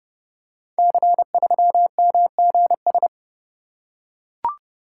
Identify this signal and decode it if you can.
Morse code, heard as a keyed tone, about 28 wpm, 700 Hz. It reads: C3MGS
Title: Code Groups - 5 Letter:Numbers - 30wpm